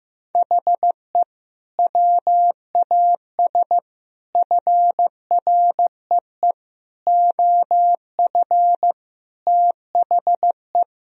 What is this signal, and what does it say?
Morse code, heard as a keyed tone, about 15 wpm, 700 Hz. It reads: HE WAS FREE OF THE
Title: Sentences from Top 1000 Words - Review Part 4 - 15wpm